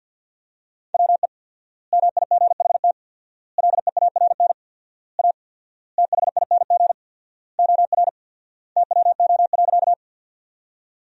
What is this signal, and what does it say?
Morse code, heard as a keyed tone, about 50 wpm, 700 Hz. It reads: WE MIGHT LEARN A THING OR TWO.
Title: 4x Word-Spacing - Sentences from Top 300 Words - 50wpm